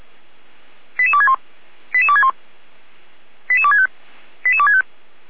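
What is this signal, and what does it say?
Five-tone selective call (ZVEI), heard as quick runs of five tones, tones of about 70 ms. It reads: 88261 88261 88265 88265
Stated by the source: ZVEI five-tone selective call received on SDR by Feuerwehr Orlamünde (fire brigade).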